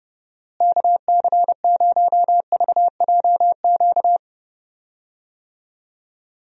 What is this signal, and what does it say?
Morse code, heard as a keyed tone, about 30 wpm, 700 Hz. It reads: KC0VJQ
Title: USA Call Signs - 6 Characters - Repeated 3x - 30wpm